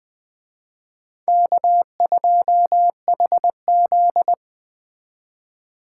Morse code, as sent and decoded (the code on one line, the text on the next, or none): -.- ..--- .... --..
K2HZ